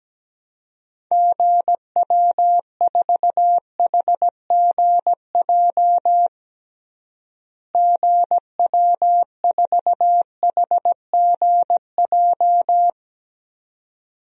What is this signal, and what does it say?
Morse code, heard as a keyed tone, about 17 wpm, 700 Hz. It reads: GW4HGJ GW4HGJ
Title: DXCC-100 Call Signs - 6 Characters - Repeated 2x - 17wpm